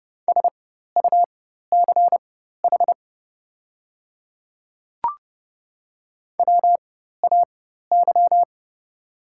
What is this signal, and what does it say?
Morse code, heard as a keyed tone, about 30 wpm, spaced as for 14 wpm, 690 Hz. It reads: SUCH WAY
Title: 4x Character Spacing - Top 300 Words - 30wpm